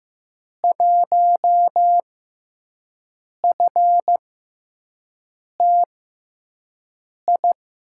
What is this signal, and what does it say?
Morse code, heard as a keyed tone, about 15 wpm, spaced as for 5 wpm, 700 Hz. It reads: 1FTI
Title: USA Call Signs - 6 Characters - 6x Character Spacing - 15wpm